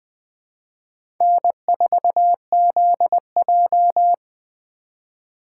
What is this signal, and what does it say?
Morse code, heard as a keyed tone, about 20 wpm, 700 Hz. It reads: N4ZJ
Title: USA Call Signs - 4x Character Spacing - 20wpm